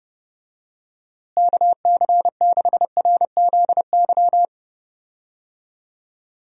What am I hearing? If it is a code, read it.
Morse code, heard as a keyed tone, about 30 wpm, 700 Hz. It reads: KC6RZY
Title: USA Call Signs - 6 Characters - Repeated 3x - 30wpm